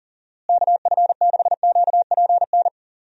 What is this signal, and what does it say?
Morse code, heard as a keyed tone, about 37 wpm, 700 Hz. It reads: KF6QPN
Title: USA Call Signs - 6 Characters - Repeated 2x - 40wpm